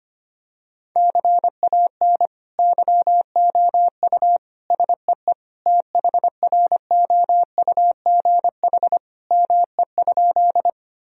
Morse code, heard as a keyed tone, about 25 wpm, 700 Hz. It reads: CAN YOU SEE THROUGH ME?
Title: Sentences from Top 1000 Words - Review Part 1 - 25wpm